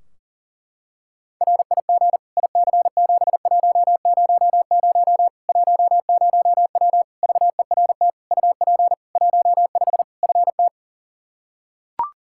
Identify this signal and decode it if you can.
Morse code, heard as a keyed tone, about 40 wpm, 700 Hz. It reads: RIG IC7100 10W VERT UP 15 FT
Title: Practice QSO - Exchange 2 - Info - 40wpm